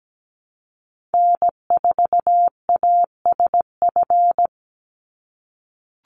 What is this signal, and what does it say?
Morse code, heard as a keyed tone, about 17 wpm, 700 Hz. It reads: N4ASF
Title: Call Signs - 17wpm